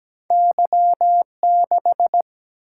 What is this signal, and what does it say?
Morse code, heard as a keyed tone, about 17 wpm, 700 Hz. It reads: Y6
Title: Single-Letter Single-Number 17wpm